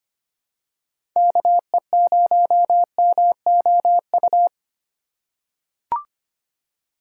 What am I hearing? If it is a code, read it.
Morse code, heard as a keyed tone, about 25 wpm, 700 Hz. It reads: KE0MOU
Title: USA Call Signs - 6 Characters - 4x Character Spacing - 25wpm